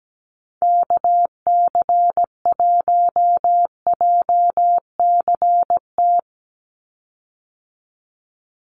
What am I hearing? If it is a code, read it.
Morse code, heard as a keyed tone, about 17 wpm, 700 Hz. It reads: KC1JCT
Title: USA Call Signs 17wpm